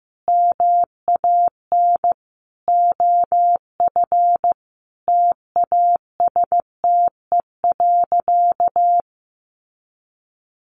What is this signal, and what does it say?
Morse code, heard as a keyed tone, about 15 wpm, 700 Hz. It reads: MAN OF TASTE.
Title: Sentences from Top 1100 Words 15wpm